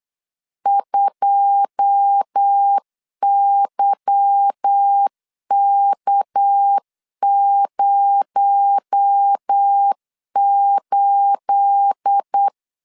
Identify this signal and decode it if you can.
Morse code, heard as a keyed tone, about 8 wpm, 800 Hz. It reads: 2YK08
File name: LEZIONI / 10 SIGNS / 10A SLASH 8wpm 3452